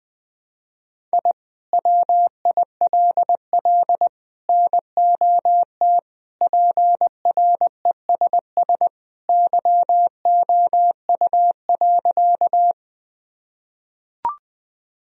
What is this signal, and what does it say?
Morse code, heard as a keyed tone, about 20 wpm, 700 Hz. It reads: I WILL NOT PRESS YOU.
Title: Sentences from Top 1000 Words - Review Part 1 - 20wpm